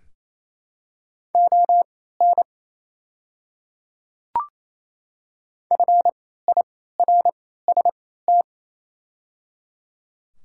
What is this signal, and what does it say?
Morse code, heard as a keyed tone, about 28 wpm, spaced as for 16 wpm, 700 Hz. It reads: ON FIRST
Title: 3x Character Spacing - Top 200 Words - 28wpm